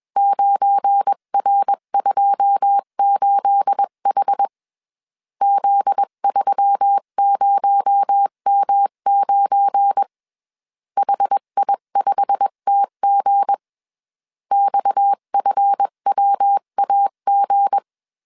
Morse code, heard as a keyed tone, about 21 wpm, 800 Hz. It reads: 9R28H Z30M9 HI5TG XFWAG